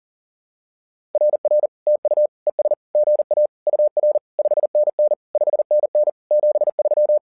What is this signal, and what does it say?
Morse code, heard as a keyed tone, about 40 wpm, 600 Hz. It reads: RR TU ES GA UR 5NN 5NN 73
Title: SOTA QSO Practice 40wpm